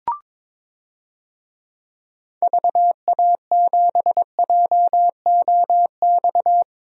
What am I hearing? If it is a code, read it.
Morse code, heard as a keyed tone, about 22 wpm, 700 Hz. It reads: VA7JOX
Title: DXCC-100 Call Signs - 6 Characters - 22wpm